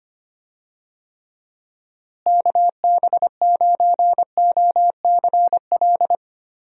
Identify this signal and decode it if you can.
Morse code, heard as a keyed tone, about 25 wpm, 700 Hz. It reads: KB9OCL
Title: USA Call Signs - 6 Characters - Repeated 3x - 25wpm